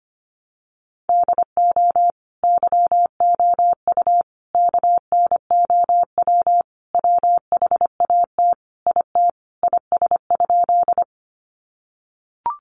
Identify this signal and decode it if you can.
Morse code, heard as a keyed tone, about 25 wpm, 700 Hz. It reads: DO YOU KNOW WHAT IT IS?
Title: Sentences from Top 100 Words 25wpm